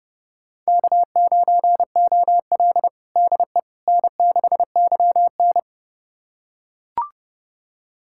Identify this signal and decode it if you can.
Morse code, heard as a keyed tone, about 30 wpm, 700 Hz. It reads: K9OL DE N6YN
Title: USA Callsign DE Callsign - 4 Characters - 30wpm